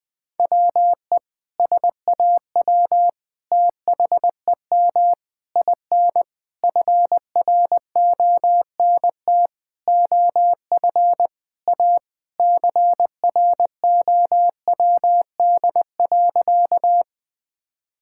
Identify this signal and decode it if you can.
Morse code, heard as a keyed tone, about 21 wpm, 700 Hz. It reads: WE SAW THEM IN FRONT OF A CROWD.
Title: Sentences with 4-Word Phrases - 20wpm